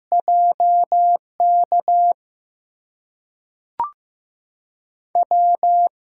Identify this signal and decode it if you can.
Morse code, heard as a keyed tone, about 15 wpm, spaced as for 5 wpm, 700 Hz. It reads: JKW
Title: USA Call Signs - 8x Character Spacing - 15wpm